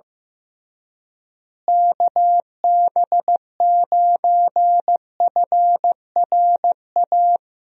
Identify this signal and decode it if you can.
Morse code, heard as a keyed tone, about 15 wpm, 700 Hz. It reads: KB9FRA
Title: USA Call Signs - 6 Characters - Repeated 3x - 15wpm